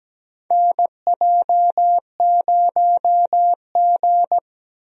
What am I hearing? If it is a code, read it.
Morse code, heard as a keyed tone, about 17 wpm, 700 Hz. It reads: NJ0G
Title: USA Call Signs - 4 Characters - 17wpm